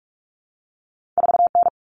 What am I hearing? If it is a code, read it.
Morse code, heard as a keyed tone, about 46 wpm, 700 Hz. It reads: 4N
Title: Common Cut Numbers - 2-Digit 45wpm